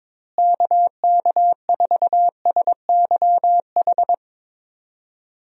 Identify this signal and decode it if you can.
Morse code, heard as a keyed tone, about 22 wpm, 700 Hz. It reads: KK4SYH
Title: USA Call Signs - 6 Characters - Repeated 3x - 22wpm